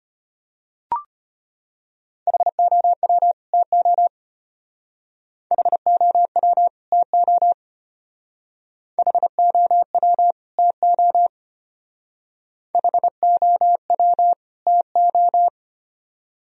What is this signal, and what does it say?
Morse code, heard as a keyed tone, about 34 wpm, 700 Hz. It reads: HOW TO HOW TO HOW TO HOW TO